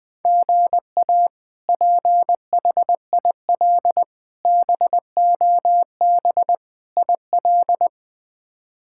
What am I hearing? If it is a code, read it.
Morse code, heard as a keyed tone, about 20 wpm, 700 Hz. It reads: GA PHIL BOB IL
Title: K1USN SST QSO Practice - 20wpm